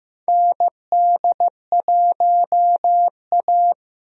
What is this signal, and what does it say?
Morse code, heard as a keyed tone, about 15 wpm, 700 Hz. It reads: ND1A
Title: USA Call Signs - 4 Characters - Repeated 3x - 15wpm